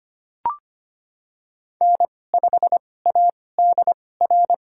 Morse code, heard as a keyed tone, about 25 wpm, spaced as for 18 wpm, 700 Hz. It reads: N5ADR
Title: USA Call Signs - 2x Character Spacing - 25wpm